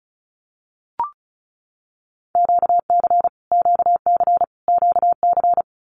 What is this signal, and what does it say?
Morse code, heard as a keyed tone, about 34 wpm, 700 Hz. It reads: QC QC QC
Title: Canadian Provinces and Territories - Repeated 3x - 35wpm